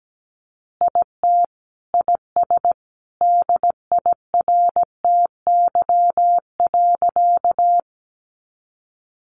Morse code, heard as a keyed tone, about 17 wpm, 700 Hz. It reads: IT IS DIRTY.
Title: Additional Easy Phrases 17wpm